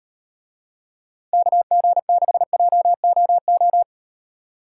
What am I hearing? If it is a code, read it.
Morse code, heard as a keyed tone, about 37 wpm, 700 Hz. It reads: KG6JOO